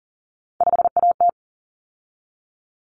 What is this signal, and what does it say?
Morse code, heard as a keyed tone, about 40 wpm, 700 Hz. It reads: 5AT
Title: Common Cut Numbers - 3-Digit 40wpm